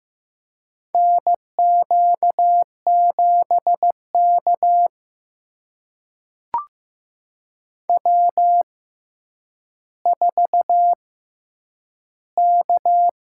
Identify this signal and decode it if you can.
Morse code, heard as a keyed tone, about 15 wpm, spaced as for 9 wpm, 700 Hz. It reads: NQ7K W4K
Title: USA Call Signs - 6x Character Spacing - 15wpm